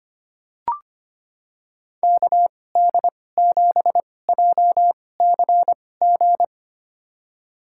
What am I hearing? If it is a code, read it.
Morse code, heard as a keyed tone, about 25 wpm, spaced as for 18 wpm, 700 Hz. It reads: KD7JCG
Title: USA Call Signs - 6 Characters - 2x Character Spacing - 25wpm